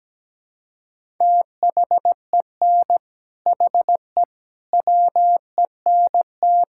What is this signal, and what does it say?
Morse code, heard as a keyed tone, about 17 wpm, 700 Hz. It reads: THEN HE WENT
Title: Sentences from Top 1000 Words - Review Part 5 - 17wpm